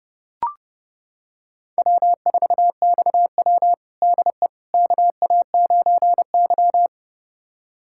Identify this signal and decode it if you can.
Morse code, heard as a keyed tone, about 29 wpm, 700 Hz. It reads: W4XW DE KA9Y
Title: USA Callsign DE Callsign - 4 Characters - 30wpm